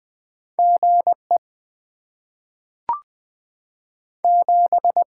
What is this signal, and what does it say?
Morse code, heard as a keyed tone, about 20 wpm, 690 Hz. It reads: GE 7
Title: Top 20 QSO Elements - 6x Character Spacing - 20wpm